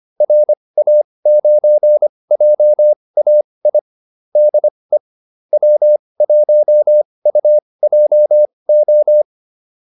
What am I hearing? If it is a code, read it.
Morse code, heard as a keyed tone, about 25 wpm, spaced as for 20 wpm, 600 Hz. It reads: RA9JAI DE W1UJO
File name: QSO401a 20